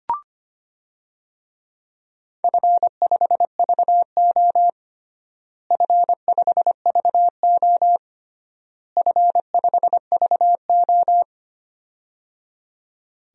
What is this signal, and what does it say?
Morse code, heard as a keyed tone, about 25 wpm, 700 Hz. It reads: F5VO F5VO F5VO
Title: DXCC-100 Call Signs - 4 Characters - Repeated 3x - 25wpm